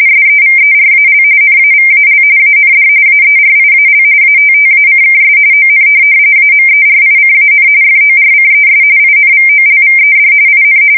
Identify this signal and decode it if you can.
RTTY, about 45.45 baud. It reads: SEVERAL NAVAL AND AERIAL ATTACKS ON AMERICAN FORCES INVOLVED IN